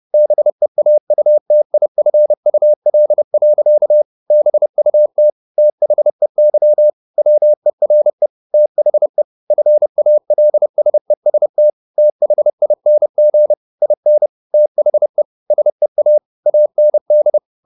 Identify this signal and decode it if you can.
Morse code, heard as a keyed tone, about 30 wpm, 600 Hz. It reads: BEAUTIFUL. BUT THEY WERE THE FALSEST THING IN THE SEA AND
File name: the old man and the sea / the old man and the sea 1 30wpm